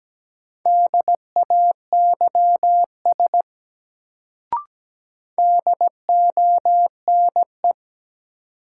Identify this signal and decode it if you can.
Morse code, heard as a keyed tone, about 17 wpm, 700 Hz. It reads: DAYS DONE
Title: Sets of 5 Words Encore - New Word Drill - 17wpm